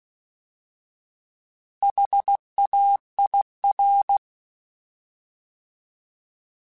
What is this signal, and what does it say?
Morse code, heard as a keyed tone, about 16 wpm, 800 Hz. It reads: HAIR
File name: Alphabetical / Alphabetical mp3s / Hs @15